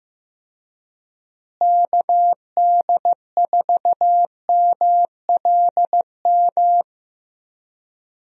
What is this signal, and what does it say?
Morse code, heard as a keyed tone, about 16 wpm, 700 Hz. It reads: KD4MLM
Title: USA Call Signs - 6 Characters - Repeated 2x - 15wpm